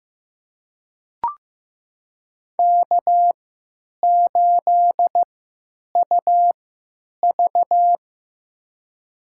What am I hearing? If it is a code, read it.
Morse code, heard as a keyed tone, about 15 wpm, spaced as for 8 wpm, 700 Hz. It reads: K8UV
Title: USA Call Signs - 4 Characters - 3x Character Spacing - 15wpm